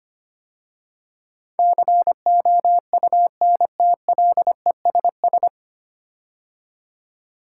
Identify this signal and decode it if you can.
Morse code, heard as a keyed tone, about 25 wpm, 700 Hz. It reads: COUNTLESS
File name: practice / 25wpm / 9 Letter Words - First 3 Letters Given - 25wpm